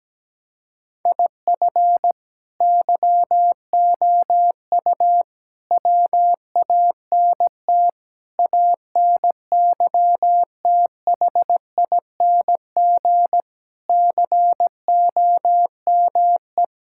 Morse code, heard as a keyed tone, about 17 wpm, 700 Hz. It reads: IF YOU WANT ANYTHING COME
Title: Sentences from Top 1000 Words - Review Part 6 - 17wpm